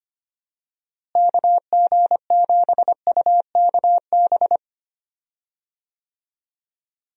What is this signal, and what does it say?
Morse code, heard as a keyed tone, about 25 wpm, 700 Hz. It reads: KG7UKB
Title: USA Call Signs - 6 Characters - Repeated 3x - 25wpm